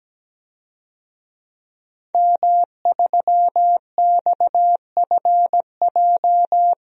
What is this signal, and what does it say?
Morse code, heard as a keyed tone, about 17 wpm, 700 Hz. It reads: M3XFJ
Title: DXCC-100 Call Signs - 5 Characters - Repeated 2x - 17wpm